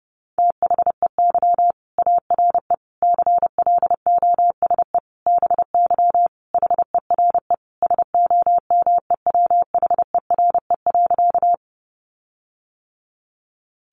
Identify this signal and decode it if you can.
Morse code, heard as a keyed tone, about 30 wpm, 700 Hz. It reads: THEY ARE CLOSE BY HERE SOMEWHERE.
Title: Sentences from Top 1400 Words 30wpm